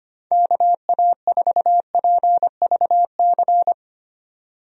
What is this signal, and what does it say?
Morse code, heard as a keyed tone, about 25 wpm, 700 Hz. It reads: KA4PVC
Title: USA Call Signs - 6 Characters - 25wpm